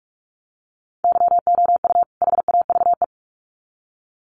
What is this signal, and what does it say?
Morse code, heard as a keyed tone, about 44 wpm, 700 Hz. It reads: YOU HAVE
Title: Sets of 2 Words 45wpm